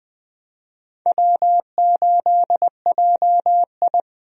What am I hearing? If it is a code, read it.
Morse code, heard as a keyed tone, about 21 wpm, 700 Hz. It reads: W8JI
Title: USA Call Signs - 4 Characters - 20wpm